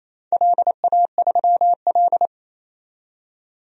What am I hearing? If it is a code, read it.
Morse code, heard as a keyed tone, about 29 wpm, 700 Hz. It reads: LA3L
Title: DXCC-100 Call Signs - 4 Characters - 28wpm